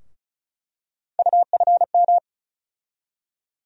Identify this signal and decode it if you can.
Morse code, heard as a keyed tone, about 36 wpm, 700 Hz. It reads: UFM
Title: USA Call Sign Postfixes - 3 Letters 35wpm